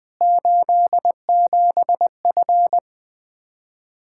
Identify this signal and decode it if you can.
Morse code, heard as a keyed tone, about 20 wpm, 700 Hz. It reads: 87F
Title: Practice QSO - Exchange 2 - Info - 20wpm